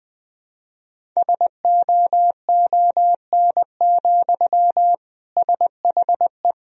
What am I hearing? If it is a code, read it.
Morse code, heard as a keyed tone, about 20 wpm, 700 Hz. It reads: SOON, SHE
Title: Sentences from Top 1000 Words - Review Part 1 - 20wpm